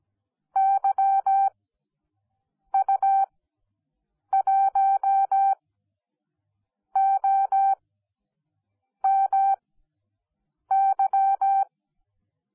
Morse code, heard as a keyed tone, about 17 wpm, spaced as for 6 wpm, 780 Hz. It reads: YU1OMY